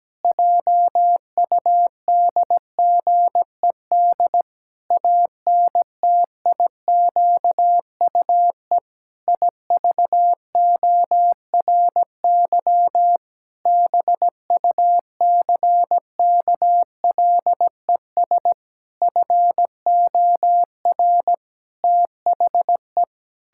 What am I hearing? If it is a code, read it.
Morse code, heard as a keyed tone, about 17 wpm, 700 Hz. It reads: JUDGED ANTIQUE IVORY BUCKLES FOR THE
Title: Pangram Challenge v2 - 17wpm